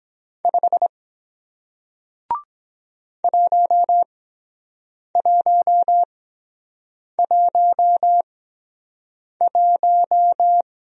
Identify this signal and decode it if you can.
Morse code, heard as a keyed tone, about 23 wpm, spaced as for 7 wpm, 700 Hz. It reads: 5 1111